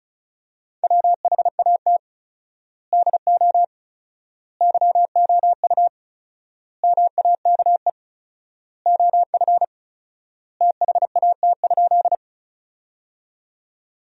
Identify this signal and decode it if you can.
Morse code, heard as a keyed tone, about 36 wpm, 700 Hz. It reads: WHAT DO YOU MAKE OF THAT?
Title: 4x Word-Spacing - Sentences from Top 100 Words - 35wpm